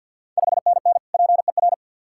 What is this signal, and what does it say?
Morse code, heard as a keyed tone, about 51 wpm, 690 Hz. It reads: 5NN PER
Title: Ontario QSO Party Exchange 50wpm